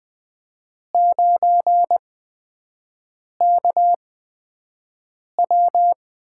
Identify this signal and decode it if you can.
Morse code, heard as a keyed tone, about 20 wpm, spaced as for 5 wpm, 700 Hz. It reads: 9KW
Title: USA Call Signs - 5 Characters - 8x Character Spacing - 20wpm